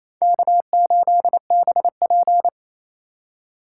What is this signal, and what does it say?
Morse code, heard as a keyed tone, about 28 wpm, 700 Hz. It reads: K8BP
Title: USA Call Signs - 2x Character Spacing - 28wpm